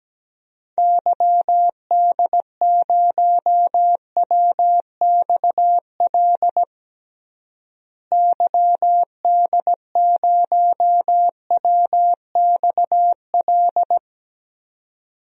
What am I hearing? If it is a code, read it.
Morse code, heard as a keyed tone, about 17 wpm, 700 Hz. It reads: YD0WXL YD0WXL
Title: DXCC-100 Call Signs - 6 Characters - Repeated 2x - 17wpm